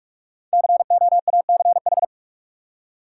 Morse code, heard as a keyed tone, about 41 wpm, 700 Hz. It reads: COACH
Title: Five Letter Words - 45wpm